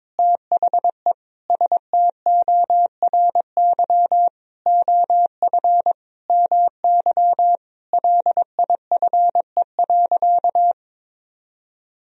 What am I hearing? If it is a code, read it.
Morse code, heard as a keyed tone, about 22 wpm, 700 Hz. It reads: THE STORY OF MY LIFE.
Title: Sentences from Top 1000 Words - Review Part 1 - 22wpm